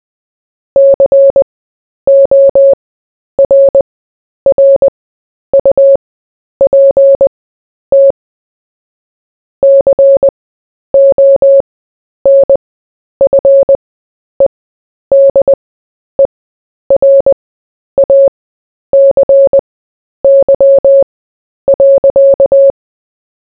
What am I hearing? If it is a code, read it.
Morse code, heard as a keyed tone, about 20 wpm, spaced as for 10 wpm, 550 Hz. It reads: CORRUPT CONFEDERACY.
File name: cw / fists / Short-Story-4-10wpm